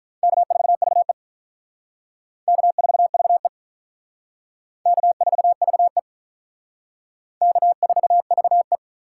Morse code, heard as a keyed tone, about 43 wpm, 700 Hz. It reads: K4VE K4VE K4VE K4VE
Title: Speed Racing - USA 1x2 Call Signs 35wpm